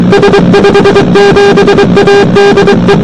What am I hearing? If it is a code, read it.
Morse code, heard as a keyed tone, about 23 wpm, 400 Hz, with noise. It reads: S57A DE